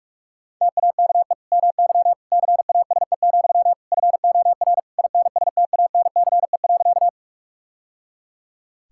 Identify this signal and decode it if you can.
Morse code, heard as a keyed tone, about 44 wpm, 700 Hz. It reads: TAKE MY CASE, FOR INSTANCE.
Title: Sentences from Top 1000 Words - Review Part 5 - 45wpm